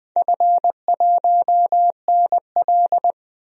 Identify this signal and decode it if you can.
Morse code, heard as a keyed tone, about 20 wpm, 700 Hz. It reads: F1NL
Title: DXCC-100 Call Signs - 4 Characters - 20wpm